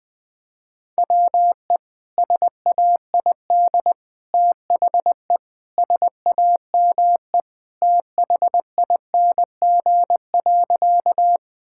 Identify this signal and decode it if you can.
Morse code, heard as a keyed tone, about 20 wpm, 700 Hz. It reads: WE SAID THE SAME THING.
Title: Sentences with 3-Word Phrases - 20wpm